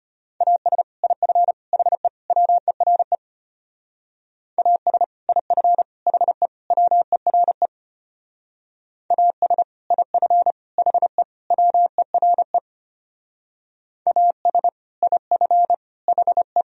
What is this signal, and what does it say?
Morse code, heard as a keyed tone, about 31 wpm, 700 Hz. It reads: AS IF HE WERE AS IF HE WERE AS IF HE WERE AS IF HE
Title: Speed-Racing - Sets of 4 Words 25wpm